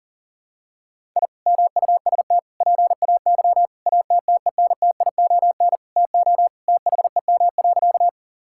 Morse code, heard as a keyed tone, about 40 wpm, 700 Hz. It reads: I MUST PAY ATTENTION TO THEM.
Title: Sentences from Top 1000 Words - Review Part 6 - 40wpm